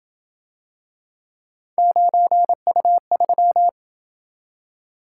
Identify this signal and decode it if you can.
Morse code, heard as a keyed tone, about 27 wpm, 700 Hz. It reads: 9U3